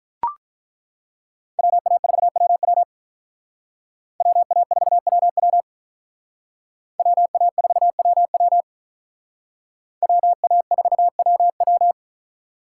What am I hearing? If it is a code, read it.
Morse code, heard as a keyed tone, about 41 wpm, 700 Hz. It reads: WA4WW WA4WW WA4WW WA4WW
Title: Speed Racing - USA 2x2 Call Signs 35wpm